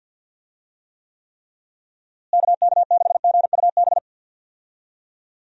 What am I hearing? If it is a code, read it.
Morse code, heard as a keyed tone, about 51 wpm, 700 Hz. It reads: KK6GUB
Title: USA Call Signs - 6 Characters - Repeated 3x - 50wpm